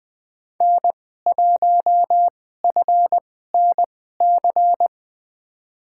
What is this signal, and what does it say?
Morse code, heard as a keyed tone, about 20 wpm, spaced as for 14 wpm, 700 Hz. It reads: N1FNC
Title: USA Call Signs - 5 Characters - 2x Character Spacing - 20wpm